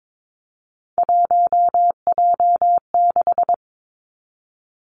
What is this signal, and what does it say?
Morse code, heard as a keyed tone, about 22 wpm, 700 Hz. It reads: 1J6
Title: Three Challenging Letter-Number Combinations 22wpm